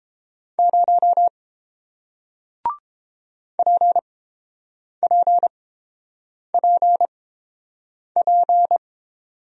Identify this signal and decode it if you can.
Morse code, heard as a keyed tone, about 31 wpm, spaced as for 7 wpm, 700 Hz. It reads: 0 PPPP